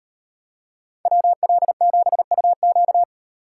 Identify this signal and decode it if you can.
Morse code, heard as a keyed tone, about 38 wpm, 700 Hz. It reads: WL7UQ